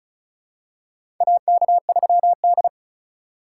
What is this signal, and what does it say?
Morse code, heard as a keyed tone, about 36 wpm, 700 Hz. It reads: AK3D
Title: USA Call Signs - 4 Characters - 4x Character Spacing - 35wpm